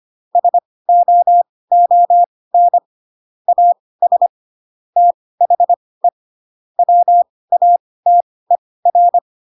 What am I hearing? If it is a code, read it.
Morse code, heard as a keyed tone, about 25 wpm, spaced as for 18 wpm, 700 Hz. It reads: SOON AS THE WATER
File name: Pinocchio / Pinocchio ch5 12 25 ik1hge